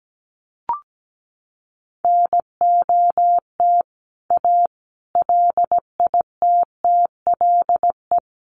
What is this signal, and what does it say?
Morse code, heard as a keyed tone, about 17 wpm, 700 Hz. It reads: NOT A LITTLE
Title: A Sound Of Thunder 17wpm